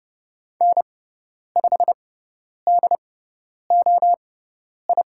Morse code, heard as a keyed tone, about 30 wpm, spaced as for 10 wpm, 700 Hz. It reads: N5DOI
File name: Lesson 16g - Letter c - Call Signs 30wpm